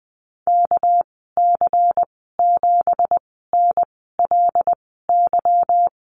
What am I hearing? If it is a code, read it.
Morse code, heard as a keyed tone, about 20 wpm, spaced as for 15 wpm, 700 Hz. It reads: KC7NLY
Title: 213 Lesson 34g - Number 8 - Call Signs 20wpm